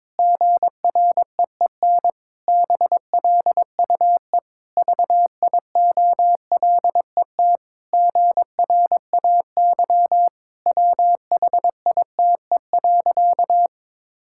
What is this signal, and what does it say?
Morse code, heard as a keyed tone, about 22 wpm, 700 Hz. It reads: GREEN BLUE VIOLET GRAY WHITE.
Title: Pangram Challenge v2 - 22wpm